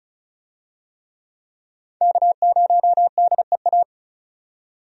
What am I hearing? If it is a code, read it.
Morse code, heard as a keyed tone, about 34 wpm, 700 Hz. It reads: K0DEA
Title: USA Call Signs - 5 Characters - 35wpm